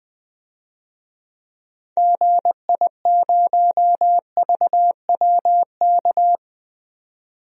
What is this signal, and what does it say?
Morse code, heard as a keyed tone, about 21 wpm, 700 Hz. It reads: GI0VWK
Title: DXCC-100 Call Signs - 6 Characters - 20wpm